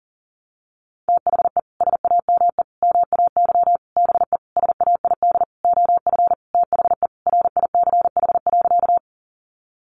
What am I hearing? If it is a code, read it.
Morse code, heard as a keyed tone, about 40 wpm, 700 Hz. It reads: THE SAME MAY BE SAID OF THE RICH.
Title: Sentences from Top 1100 Words 40wpm